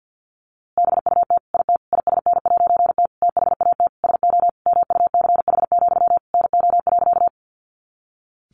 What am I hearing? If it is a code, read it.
Morse code, heard as a keyed tone, about 50 wpm, 700 Hz. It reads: BUT IT ISN'T THAT SO MUCH, NO.
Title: A Sound Of Thunder 50wpm